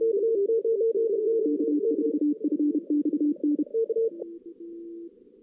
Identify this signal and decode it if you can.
Morse code, heard as a keyed tone, about 32 wpm, 300 Hz. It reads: K4FXN
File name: 2016 JAN CW NAQP